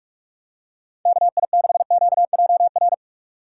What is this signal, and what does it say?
Morse code, heard as a keyed tone, about 43 wpm, 700 Hz. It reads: KI6QJR
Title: USA Call Signs - 6 Characters - 3x Character Spacing - 45wpm